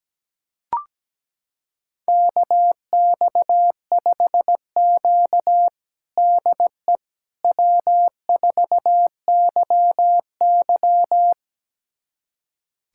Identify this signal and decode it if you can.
Morse code, heard as a keyed tone, about 17 wpm, 700 Hz. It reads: KX5Q DE W4YY
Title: USA Callsign DE Callsign - 4 Characters - 17wpm